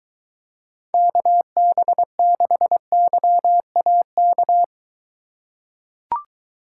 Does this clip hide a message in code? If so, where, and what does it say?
Yes: Morse code from 0.9 to 4.6 s, heard as a keyed tone, about 22 wpm, 700 Hz. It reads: KB6YAK